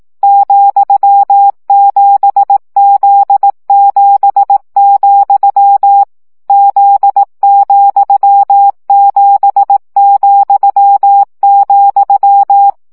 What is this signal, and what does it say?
Morse code, heard as a keyed tone, about 18 wpm, 800 Hz. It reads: ,7Z7, Z,7,,
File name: CW gyakorlas z 7 vesszo 15 baud